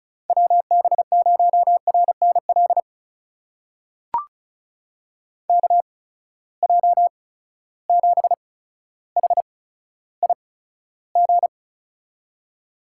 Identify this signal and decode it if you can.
Morse code, heard as a keyed tone, about 34 wpm, spaced as for 15 wpm, 700 Hz. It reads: WB0RNL KJ7HIG
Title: USA Call Signs - 6 Characters - 8x Character Spacing - 35wpm